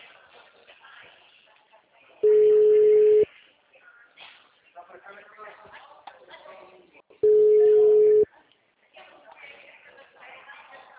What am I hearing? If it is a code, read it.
Ringback tone (Europe, 425 Hz), with speech.